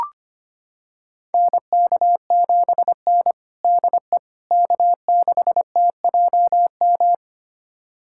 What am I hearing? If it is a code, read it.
Morse code, heard as a keyed tone, about 25 wpm, 700 Hz. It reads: NK7N DE K6TJM
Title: USA Callsign DE Callsign 25wpm